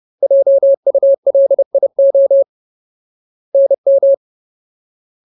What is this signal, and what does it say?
Morse code, heard as a keyed tone, about 30 wpm, 550 Hz. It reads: JULIO NM
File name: Ultimate Word List / 30WPM Ultimate